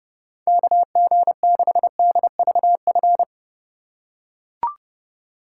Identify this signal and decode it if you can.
Morse code, heard as a keyed tone, about 30 wpm, 700 Hz. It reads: KG6DVF
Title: USA Call Signs - 3x Character Spacing - 30wpm